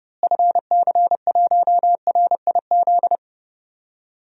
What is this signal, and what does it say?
Morse code, heard as a keyed tone, about 32 wpm, 700 Hz. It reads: FC1RIZ
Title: DXCC-100 Call Signs - 6 Characters - Repeated 2x - 30wpm